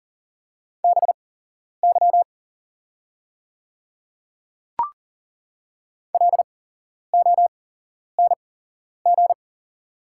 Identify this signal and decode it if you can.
Morse code, heard as a keyed tone, about 40 wpm, spaced as for 11 wpm, 700 Hz. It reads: BY LONG
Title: 8x Character Spacing - Top 200 Words - 40wpm